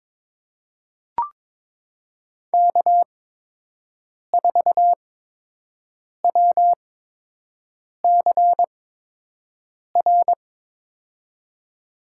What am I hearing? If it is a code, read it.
Morse code, heard as a keyed tone, about 22 wpm, spaced as for 6 wpm, 700 Hz. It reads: K4WCR
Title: USA Call Signs - 8x Character Spacing - 22wpm